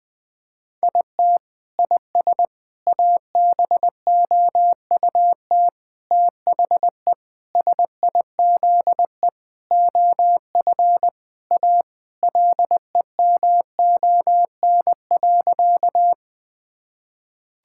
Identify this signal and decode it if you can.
Morse code, heard as a keyed tone, about 19 wpm, 700 Hz. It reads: IT IS ABOUT THE SIZE OF A LEMON.
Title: Sentences with 5-Word Phrases - 20wpm